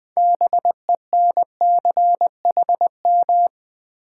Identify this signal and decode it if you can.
Morse code, heard as a keyed tone, about 20 wpm, 700 Hz. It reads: BENCHM
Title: Build A Word - 3 Letters At A Time - 9-Letter Words - 20wpm